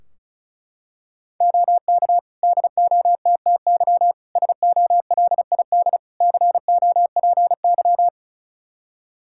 Morse code, heard as a keyed tone, about 36 wpm, 700 Hz. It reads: OK DOTTY SOLID COPY
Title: K1USN SST QSO Practice 35wpm